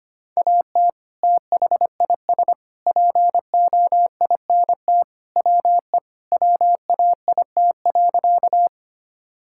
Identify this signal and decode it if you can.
Morse code, heard as a keyed tone, about 25 wpm, 700 Hz. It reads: AT THIS POINT WE WAIT.
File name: practice / 25wpm / Sentences with 3-Word Phrases - 25wpm